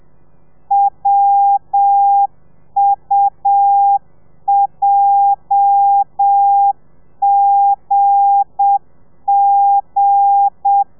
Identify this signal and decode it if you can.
Morse code, heard as a keyed tone, about 7 wpm, 800 Hz. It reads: WUJGG
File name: telegrafia / i2tfj / Lez8 30